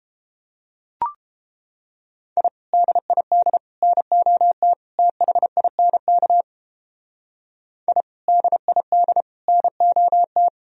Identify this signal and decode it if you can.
Morse code, heard as a keyed tone, about 29 wpm, 700 Hz. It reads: I DID NOT THINK I DID NOT